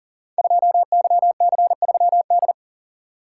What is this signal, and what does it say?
Morse code, heard as a keyed tone, about 38 wpm, 700 Hz. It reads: 2YC3D
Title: Code Groups - 5 Letter:Numbers - 40wpm